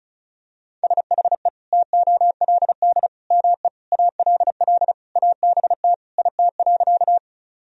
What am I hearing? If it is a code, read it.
Morse code, heard as a keyed tone, about 36 wpm, 700 Hz. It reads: SHE TOLD ME ALL ABT IT.
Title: Sentences from Top 400 Words and Abbreviations 35wpm